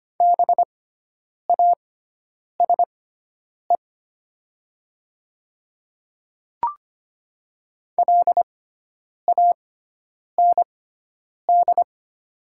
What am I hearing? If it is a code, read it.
Morse code, heard as a keyed tone, about 25 wpm, spaced as for 9 wpm, 700 Hz. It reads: BASE LAND
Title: 6x Character Spacing - Top 300 Words - 25wpm